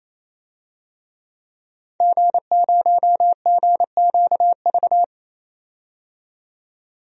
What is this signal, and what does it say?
Morse code, heard as a keyed tone, about 28 wpm, 700 Hz. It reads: G0GQV
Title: DXCC-100 Call Signs - 5 Characters - 28wpm